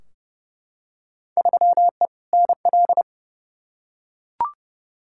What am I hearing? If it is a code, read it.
Morse code, heard as a keyed tone, about 30 wpm, 700 Hz. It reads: 3E NL
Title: Field Day Exchanges Encore - 30wpm